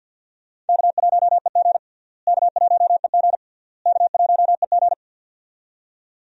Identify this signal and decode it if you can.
Morse code, heard as a keyed tone, about 48 wpm, 700 Hz. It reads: K1EG K1EG K1EG
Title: USA Call Signs - 4 Characters - Repeated 3x - 50wpm